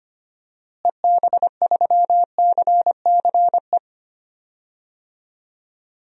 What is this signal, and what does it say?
Morse code, heard as a keyed tone, about 25 wpm, 700 Hz. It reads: EB3CCE
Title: DXCC-100 Call Signs - 6 Characters - Repeated 3x - 25wpm